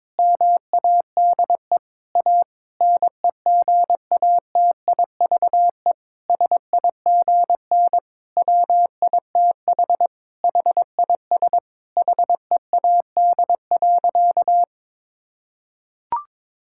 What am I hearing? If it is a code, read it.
Morse code, heard as a keyed tone, about 22 wpm, 700 Hz. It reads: MADE A NEGATIVE SIGN WITH HIS HEAD.
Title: Sentences from Top 1000 Words - Review Part 6 - 22wpm